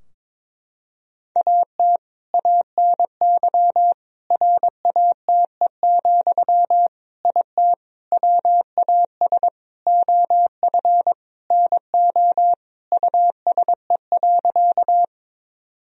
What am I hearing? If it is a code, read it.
Morse code, heard as a keyed tone, about 22 wpm, 700 Hz. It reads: AT ANY RATE, IT WAS OF NO USE.
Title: Sentences from Top 1000 Words - Review Part 4 - 22wpm